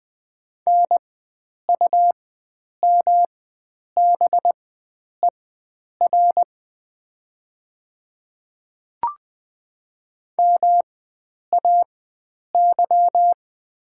Morse code, heard as a keyed tone, about 20 wpm, spaced as for 9 wpm, 700 Hz. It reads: NUMBER MAY
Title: 4x Character Spacing - Top 100 Words - 20wpm